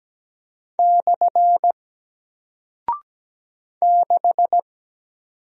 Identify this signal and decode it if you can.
Morse code, heard as a keyed tone, about 17 wpm, 700 Hz. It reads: / 6
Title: Single Character (Letter:Number:Punctuation:Prosign) Alt 17wpm